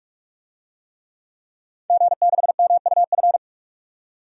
Morse code, heard as a keyed tone, about 46 wpm, 700 Hz. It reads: G6MUF
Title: DXCC-100 Call Signs - 5 Characters - 45wpm